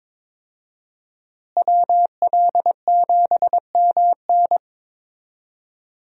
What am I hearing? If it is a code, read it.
Morse code, heard as a keyed tone, about 22 wpm, 700 Hz. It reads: WL7MN
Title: USA Call Signs - 5 Characters - Repeated 3x - 22wpm